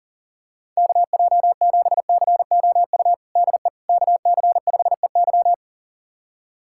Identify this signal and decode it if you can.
Morse code, heard as a keyed tone, about 40 wpm, 700 Hz. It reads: KJ7COU DE KC5EY
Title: USA Callsign DE Callsign 40wpm